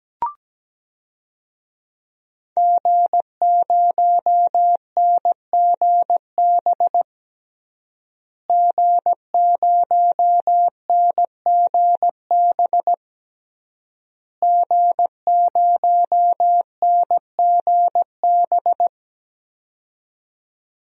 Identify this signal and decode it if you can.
Morse code, heard as a keyed tone, about 17 wpm, 700 Hz. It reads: G0NGB G0NGB G0NGB
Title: DXCC-100 Call Signs - 5 Characters - Repeated 3x - 17wpm